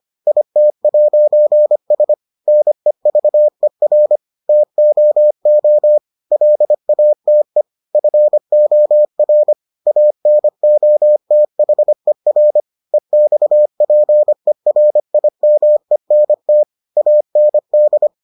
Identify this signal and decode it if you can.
Morse code, heard as a keyed tone, about 25 wpm, 600 Hz. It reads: IT'S NEVER TOO LATE FOR ANOTHER EXPERIMENT AND